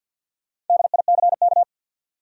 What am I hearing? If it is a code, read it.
Morse code, heard as a keyed tone, about 52 wpm, 700 Hz. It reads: DICK
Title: USA Common Names - Max 4 Letters - 50wpm